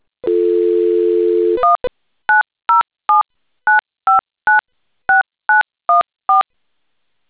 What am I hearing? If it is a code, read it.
DTMF, heard as telephone touch tones, tones of about 150 ms. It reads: 19079596914